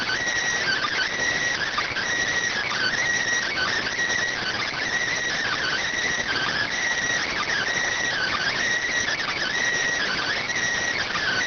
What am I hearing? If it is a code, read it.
Emergency Alert System (SAME) tones (header bursts), heard as harsh buzzing bursts. It is heard from 0.6 to 10.5 s.